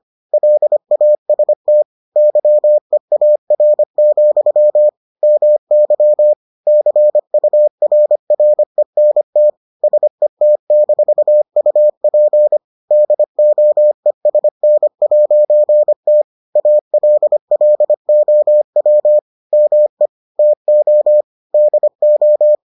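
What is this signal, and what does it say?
Morse code, heard as a keyed tone, about 25 wpm, 600 Hz. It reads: LAST YEAR, MY CURRENT SET-UP DOESN'T ALLOW ME TO DO
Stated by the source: Morse edition of the Foundations of Amateur Radio episodes.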